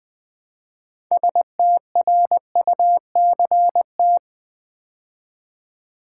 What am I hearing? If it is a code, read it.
Morse code, heard as a keyed tone, about 20 wpm, 700 Hz. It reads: STRUCT
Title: Build A Word - 3 Letters At A Time - 9-Letter Words - 20wpm